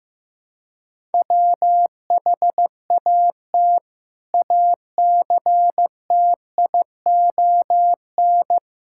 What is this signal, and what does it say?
Morse code, heard as a keyed tone, about 15 wpm, 700 Hz. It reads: WHAT ACTION
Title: Sentences from Top 1000 Words - Review Part 6 - 15wpm